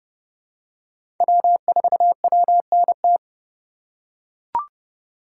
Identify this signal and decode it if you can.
Morse code, heard as a keyed tone, about 30 wpm, 700 Hz. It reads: W4WNT
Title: K1USN SST QSO Practice - 30wpm